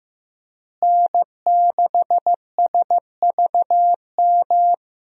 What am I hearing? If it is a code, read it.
Morse code, heard as a keyed tone, about 15 wpm, 700 Hz. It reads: N6SVM
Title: USA Call Signs - 5 Characters - Repeated 3x - 15wpm